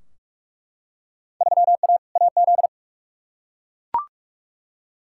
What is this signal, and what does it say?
Morse code, heard as a keyed tone, about 42 wpm, 700 Hz. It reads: 3A AZ
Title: Field Day Exchanges - 45wpm